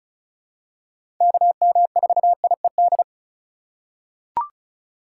Morse code, heard as a keyed tone, about 36 wpm, 700 Hz. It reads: KM4IED
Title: USA Call Signs - 6 Characters - 6x Character Spacing - 35wpm